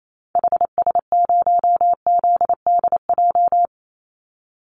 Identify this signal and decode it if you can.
Morse code, heard as a keyed tone, about 28 wpm, 700 Hz. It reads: HS0ZDJ
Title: Call Signs - 28wpm